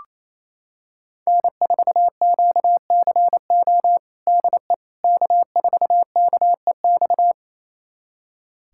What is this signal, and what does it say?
Morse code, heard as a keyed tone, about 28 wpm, 700 Hz. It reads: N4QCO DE K4KEX
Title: USA Callsign DE Callsign 28wpm